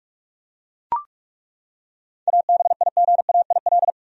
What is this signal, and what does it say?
Morse code, heard as a keyed tone, about 41 wpm, 700 Hz. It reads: ABIGAIL
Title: USA Common Names - 45wpm